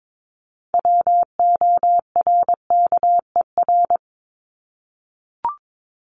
Morse code, heard as a keyed tone, about 22 wpm, 700 Hz. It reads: WORKER
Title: ES:S Suffix - Top 100 Words with Base Words 22wpm